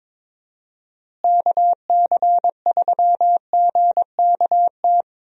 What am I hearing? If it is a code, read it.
Morse code, heard as a keyed tone, about 22 wpm, 700 Hz. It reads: KC3GKT
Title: USA Call Signs - 6 Characters - Repeated 3x - 22wpm